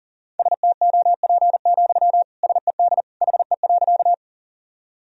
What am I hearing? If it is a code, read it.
Morse code, heard as a keyed tone, about 40 wpm, 700 Hz. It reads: STOP, SED HE.
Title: Sentences from Top 300 Words and Abbreviations 40wpm